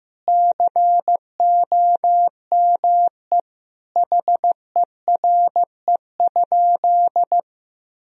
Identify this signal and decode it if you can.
Morse code, heard as a keyed tone, about 15 wpm, 700 Hz. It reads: COME HERE?
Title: Sentences from Top 1000 Words - Review Part 4 - 15wpm